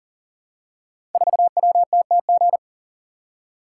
Morse code, heard as a keyed tone, about 43 wpm, 700 Hz. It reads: 4WTTG
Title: Code Groups - 5 Letter:Numbers - 40wpm